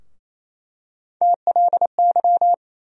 Morse code, heard as a keyed tone, about 28 wpm, 700 Hz. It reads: TLY
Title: USA Call Sign Postfixes - 3 Letters 28wpm